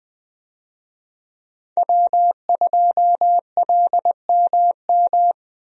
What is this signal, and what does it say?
Morse code, heard as a keyed tone, about 21 wpm, 700 Hz. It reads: W2LMM
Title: USA Call Signs - 5 Characters - Repeated 2x - 20wpm